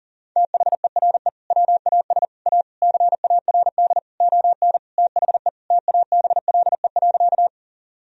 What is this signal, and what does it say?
Morse code, heard as a keyed tone, about 40 wpm, 700 Hz. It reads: THERE WAS A CARD ON THE TABLE.
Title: Sentences from Top 1000 Words - Review Part 4 - 40wpm